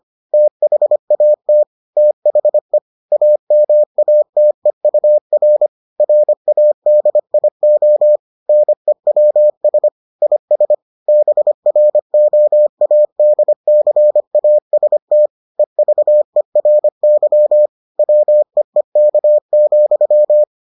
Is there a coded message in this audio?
Yes: Morse code from 0.3 to 20.4 s, heard as a keyed tone, about 25 wpm, 600 Hz. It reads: THAT THE AMATEUR RADIO NEWS IS BROADCAST EVERY WEEK,